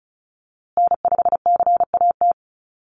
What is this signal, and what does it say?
Morse code, heard as a keyed tone, about 34 wpm, 700 Hz. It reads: N5CAT
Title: USA Call Signs Repeated 2x 35wpm